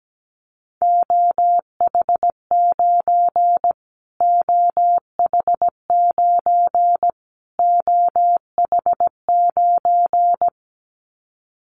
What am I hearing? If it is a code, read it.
Morse code, heard as a keyed tone, about 17 wpm, 700 Hz. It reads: OH9 OH9 OH9
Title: Callsign Prefixes for Top 25 DXCC - Repeated 3x - 17wpm